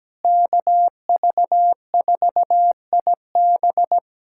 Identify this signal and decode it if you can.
Morse code, heard as a keyed tone, about 17 wpm, 700 Hz. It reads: KV4IB
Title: USA Callsign DE Callsign - 5 Characters - 17wpm